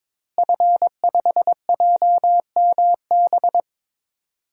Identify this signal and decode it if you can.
Morse code, heard as a keyed tone, about 22 wpm, 700 Hz. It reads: F5JMB
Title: DXCC-100 Call Signs - 5 Characters - Repeated 2x - 22wpm